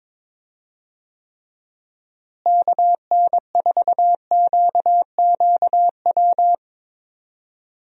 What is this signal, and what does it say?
Morse code, heard as a keyed tone, about 22 wpm, 700 Hz. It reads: KN4QQW
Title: USA Call Signs - 6 Characters - Repeated 2x - 22wpm